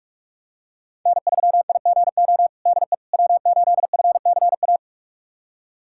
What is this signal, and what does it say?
Morse code, heard as a keyed tone, about 46 wpm, 700 Hz. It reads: N3IGO DE W8FCA
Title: USA Callsign DE Callsign - 5 Characters - 45wpm